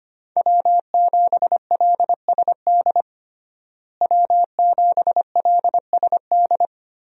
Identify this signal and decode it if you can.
Morse code, heard as a keyed tone, about 25 wpm, 700 Hz. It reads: W7LSD W7LSD
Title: USA Call Signs - 5 Characters - Repeated 3x - 25wpm